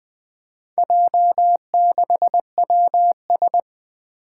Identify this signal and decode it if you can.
Morse code, heard as a keyed tone, about 20 wpm, 700 Hz. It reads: J6WS
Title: Four Challenging Letter-Number Combinations 20wpm